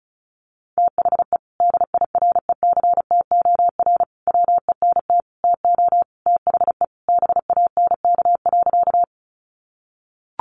Morse code, heard as a keyed tone, about 34 wpm, 700 Hz. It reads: THE DIRECTOR WENT TO THE BANK.
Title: Sentences from Top 1300 Words 35wpm